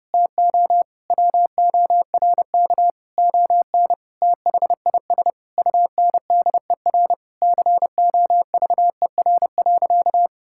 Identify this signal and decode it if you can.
Morse code, heard as a keyed tone, about 28 wpm, 700 Hz. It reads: TO WORK ON THIS UNDER COVER.
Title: Sentences from Top 1000 Words - Review Part 6 - 30wpm